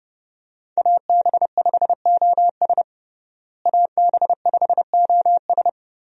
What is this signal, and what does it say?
Morse code, heard as a keyed tone, about 32 wpm, 700 Hz. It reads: AB5OS AB5OS
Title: USA Call Signs - 5 Characters - Repeated 2x - 30wpm